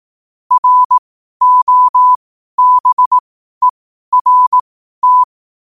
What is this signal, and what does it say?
Morse code, heard as a keyed tone, about 18 wpm, spaced as for 12 wpm, 1000 Hz. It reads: ROBERT
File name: qotdfiles / 12wpm